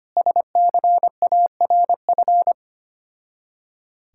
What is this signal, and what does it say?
Morse code, heard as a keyed tone, about 25 wpm, 700 Hz. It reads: SCARF
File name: practice / 25wpm / Difficult Words with Hints - 25wpm